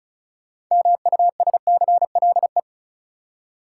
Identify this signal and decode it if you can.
Morse code, heard as a keyed tone, about 36 wpm, 700 Hz. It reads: MUSCLE
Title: Six Letter Words - 35wpm